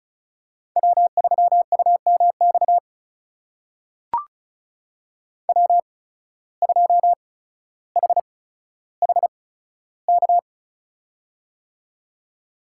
Morse code, heard as a keyed tone, about 36 wpm, spaced as for 15 wpm, 700 Hz. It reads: W3UMX W2HHK
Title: USA Call Signs - 5 Characters - 8x Character Spacing - 35wpm